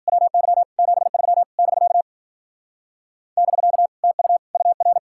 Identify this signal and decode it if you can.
Morse code, heard as a keyed tone, about 50 wpm, 700 Hz. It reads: WY 73 <BK> <BK> TU UR
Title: Speed-Racing Common QSO Phrases Encore - 40wpm